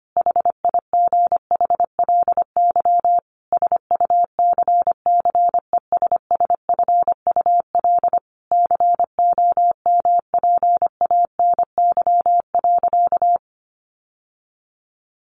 Morse code, heard as a keyed tone, about 25 wpm, 690 Hz. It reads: HIGHLY SUCCESSFUL COMPANY.
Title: Sentences from Top 1100 Words 25wpm